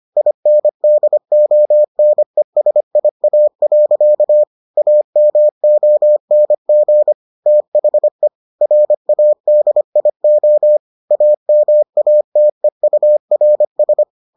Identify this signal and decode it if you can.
Morse code, heard as a keyed tone, about 25 wpm, 600 Hz. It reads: INDONESIA. AMONG THE RADIO AMATEURS